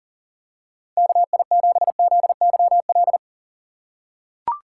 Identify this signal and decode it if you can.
Morse code, heard as a keyed tone, about 40 wpm, 700 Hz. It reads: KI7ZYL
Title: USA Call Signs - 6 Characters - 4x Character Spacing - 40wpm